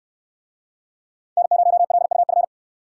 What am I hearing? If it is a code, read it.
Morse code, heard as a keyed tone, about 64 wpm, 700 Hz. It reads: N0LUF